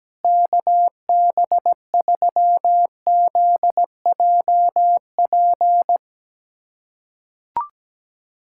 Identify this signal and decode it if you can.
Morse code, heard as a keyed tone, about 17 wpm, 700 Hz. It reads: KB3ZJP
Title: USA Callsign DE Callsign 17wpm